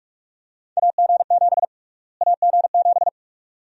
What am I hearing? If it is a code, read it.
Morse code, heard as a keyed tone, about 46 wpm, 700 Hz. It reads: AG7 AG7
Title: USA Call Sign Prefixes - Repeated 2x - 45wpm